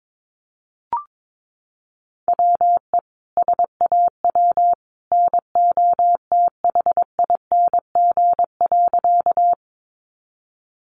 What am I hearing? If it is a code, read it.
Morse code, heard as a keyed tone, about 22 wpm, 700 Hz. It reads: WE SAW NOTHING.
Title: A Sound Of Thunder 22wpm